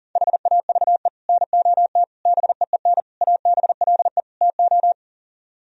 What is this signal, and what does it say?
Morse code, heard as a keyed tone, about 52 wpm, 700 Hz. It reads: HAVE NOT BEEN ABLE TO